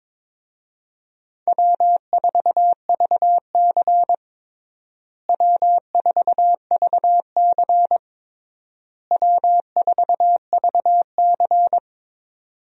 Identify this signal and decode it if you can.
Morse code, heard as a keyed tone, about 22 wpm, 700 Hz. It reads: W4VC W4VC W4VC
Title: USA Call Signs - 4 Characters - Repeated 3x - 22wpm